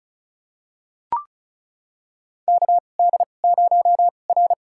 Morse code, heard as a keyed tone, about 36 wpm, spaced as for 25 wpm, 700 Hz. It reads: KD0R
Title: USA Call Signs - 4 Characters - 2x Character Spacing - 35wpm